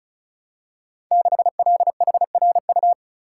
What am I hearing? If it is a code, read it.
Morse code, heard as a keyed tone, about 36 wpm, 700 Hz. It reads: 6LHRU
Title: Code Groups - 5 Letter:Numbers - 35wpm